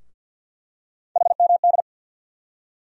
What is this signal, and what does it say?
Morse code, heard as a keyed tone, about 52 wpm, 700 Hz. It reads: HMD
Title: USA Call Sign Postfixes - 3 Letters 50wpm